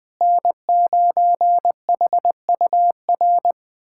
Morse code, heard as a keyed tone, about 21 wpm, 700 Hz. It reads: N9HUR
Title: USA Callsign DE Callsign - 5 Characters - 20wpm